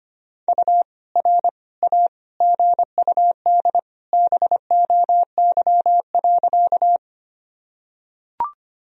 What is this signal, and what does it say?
Morse code, heard as a keyed tone, about 25 wpm, 700 Hz. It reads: U R A GUD BOY.
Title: Sentences from Top 200 Words and Abbreviations 25wpm